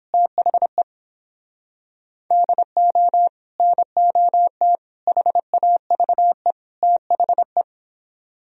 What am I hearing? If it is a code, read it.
Morse code, heard as a keyed tone, about 29 wpm, 700 Hz. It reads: THE DO NOT HAVE THE